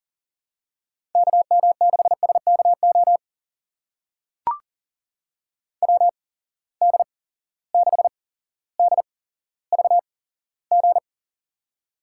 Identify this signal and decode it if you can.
Morse code, heard as a keyed tone, about 40 wpm, spaced as for 17 wpm, 700 Hz. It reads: KM6SKO WD6DVG
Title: USA Call Signs - 6 Characters - 8x Character Spacing - 40wpm